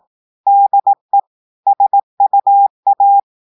Morse code, heard as a keyed tone, about 18 wpm, 800 Hz. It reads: DE SUA
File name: audios / portugais / portugais texte en clair 1 vitesse 18 mots minutes